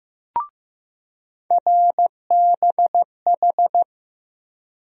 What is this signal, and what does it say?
Morse code, heard as a keyed tone, about 15 wpm, 700 Hz. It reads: RBH
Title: USA Call Sign Postfixes - 3 Letters 15wpm